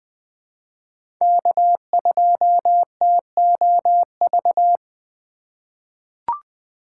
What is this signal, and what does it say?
Morse code, heard as a keyed tone, about 21 wpm, 700 Hz. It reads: K2TOV
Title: USA Call Signs - 5 Characters - 2x Character Spacing - 20wpm